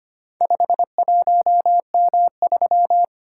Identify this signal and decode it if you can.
Morse code, heard as a keyed tone, about 25 wpm, 690 Hz. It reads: 51M3
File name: practice / 25wpm / Four Letter-Number Combinations - 25wpm